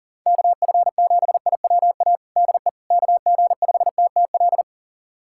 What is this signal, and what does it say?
Morse code, heard as a keyed tone, about 37 wpm, 700 Hz. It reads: KF7IWA DE KG5TTL
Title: USA Callsign DE Callsign 40wpm